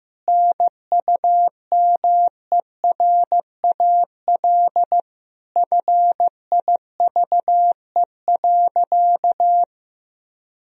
Morse code, heard as a keyed tone, about 15 wpm, 700 Hz. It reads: NUMERAL FIVE.
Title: Sentences from Top 1000 Words - Review Part 2 - 15wpm